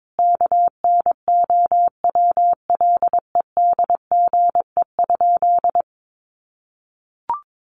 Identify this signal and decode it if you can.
Morse code, heard as a keyed tone, about 22 wpm, 700 Hz. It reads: KNOWLEDGE?
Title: Sentences from Top 600 Words 22wpm